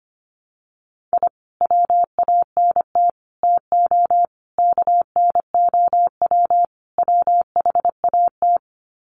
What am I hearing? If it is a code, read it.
Morse code, heard as a keyed tone, about 25 wpm, 700 Hz. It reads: I WANT TO KNOW WHAT
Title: Sets of 5 Words Encore2 25wpm